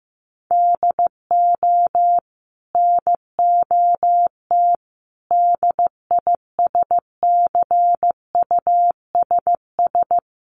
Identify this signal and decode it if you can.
Morse code, heard as a keyed tone, about 15 wpm, 700 Hz. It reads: DO NOT DISCUSS
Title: Sentences from Top 1200 Words 15wpm